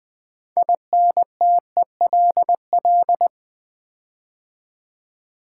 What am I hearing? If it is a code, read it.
Morse code, heard as a keyed tone, about 20 wpm, 700 Hz. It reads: INTELL
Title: Build A Word - 6 Letters At A Time - 12-Letter Words - 20wpm